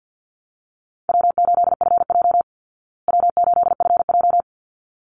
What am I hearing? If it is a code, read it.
Morse code, heard as a keyed tone, about 48 wpm, 700 Hz. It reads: W8FJ W8FJ
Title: Easy Call Signs Repeated 2x - 50wpm